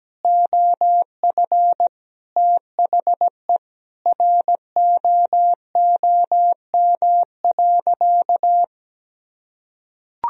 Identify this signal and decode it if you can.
Morse code, heard as a keyed tone, about 17 wpm, 700 Hz. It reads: OF THE ROOM.
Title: Sentences from Top 500 Words and Abbreviations 17wpm